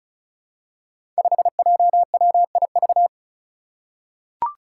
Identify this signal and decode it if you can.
Morse code, heard as a keyed tone, about 36 wpm, 700 Hz. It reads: 5JWIV
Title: Five Challenging Letter-Number Combinations 35wpm